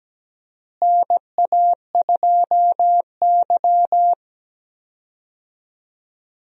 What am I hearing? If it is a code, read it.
Morse code, heard as a keyed tone, about 17 wpm, 700 Hz. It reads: NA2Y
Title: USA Call Signs - 4 Characters - Repeated 3x - 17wpm